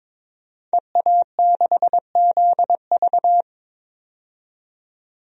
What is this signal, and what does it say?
Morse code, heard as a keyed tone, about 22 wpm, 700 Hz. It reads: EA6ZV
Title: DXCC-100 Call Signs - 5 Characters - 22wpm